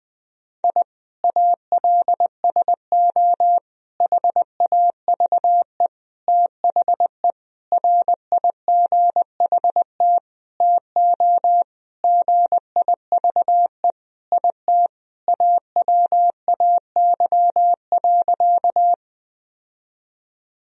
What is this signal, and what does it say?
Morse code, heard as a keyed tone, about 20 wpm, 700 Hz. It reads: I ALSO HAVE THE RIGHT TO GIVE IT AWAY.
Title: Sentences from Top 1000 Words - Review Part 1 - 20wpm